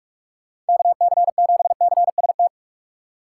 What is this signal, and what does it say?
Morse code, heard as a keyed tone, about 43 wpm, 700 Hz. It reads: KC7CST
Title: USA Call Signs - 6 Characters - 2x Character Spacing - 45wpm